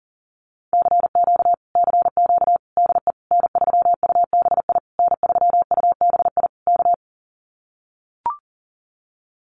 Morse code, heard as a keyed tone, about 40 wpm, 700 Hz. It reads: CQ CQ DE N3UBI N3UBI K
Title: Common QSO Phrases Encore 40wpm